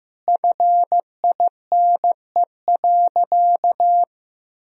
Morse code, heard as a keyed tone, about 15 wpm, 700 Hz. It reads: FINE.
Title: Sentences from Top 1000 Words - Review Part 1 - 15wpm